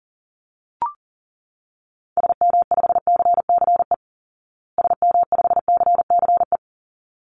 Morse code, heard as a keyed tone, about 40 wpm, 700 Hz. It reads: SM5CCE SM5CCE
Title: Call Signs Repeated 3x - 40wpm